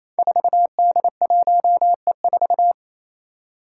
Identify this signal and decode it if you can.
Morse code, heard as a keyed tone, about 28 wpm, 700 Hz. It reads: 4D1E4
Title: Five Challenging Letter-Number Combinations 28wpm